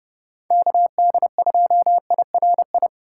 Morse code, heard as a keyed tone, about 28 wpm, 700 Hz. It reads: KD2IRI
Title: Speed Racing - USA 2x3 Call Signs 30wpm